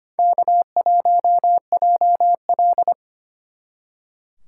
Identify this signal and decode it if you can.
Morse code, heard as a keyed tone, about 30 wpm, 700 Hz. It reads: K1JL
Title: Speed Racing - USA 1x2 Call Signs 25wpm